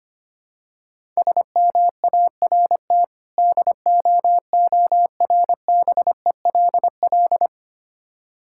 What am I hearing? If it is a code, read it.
Morse code, heard as a keyed tone, about 25 wpm, 690 Hz. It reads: SMART DOORBELL
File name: practice / 25wpm / Difficult Words with Hints - 25wpm